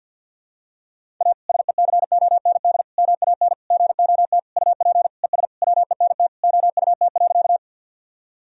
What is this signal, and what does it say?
Morse code, heard as a keyed tone, about 52 wpm, 700 Hz. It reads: A SECOND MAN GOT UP ES WENT OUT.
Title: Sentences from Top 300 Words and Abbreviations 50wpm